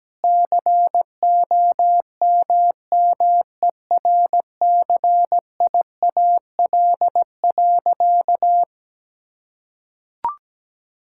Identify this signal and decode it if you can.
Morse code, heard as a keyed tone, about 17 wpm, 700 Hz. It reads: COMMERCIAL.
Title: Sentences from Top 1000 Words - Review Part 5 - 17wpm